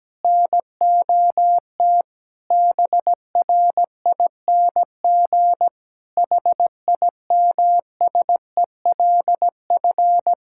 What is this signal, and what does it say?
Morse code, heard as a keyed tone, about 17 wpm, 700 Hz. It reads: NOT BRING HIMSELF
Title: Sentences from Top 1600 Words 17wpm